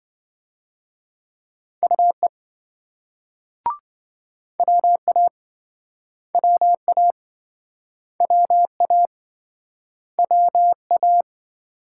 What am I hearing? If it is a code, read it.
Morse code, heard as a keyed tone, about 28 wpm, 700 Hz. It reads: UE WA WA WA WA